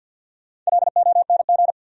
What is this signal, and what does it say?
Morse code, heard as a keyed tone, about 68 wpm, spaced as for 58 wpm, 700 Hz. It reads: LONG
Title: Speed-Racing Top 200 Words 50wpm